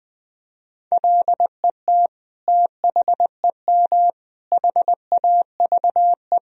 Morse code, heard as a keyed tone, about 20 wpm, 700 Hz. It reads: LET THEM HAVE
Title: Sentences from Top 1000 Words - Review Part 3 - 20wpm